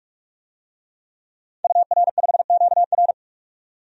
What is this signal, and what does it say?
Morse code, heard as a keyed tone, about 46 wpm, 700 Hz. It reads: UR5QR
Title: DXCC-100 Call Signs - 5 Characters - 45wpm